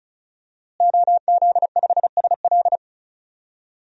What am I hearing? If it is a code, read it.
Morse code, heard as a keyed tone, about 34 wpm, 700 Hz. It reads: OZ5SL
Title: DXCC-100 Call Signs - 5 Characters - Repeated 2x - 35wpm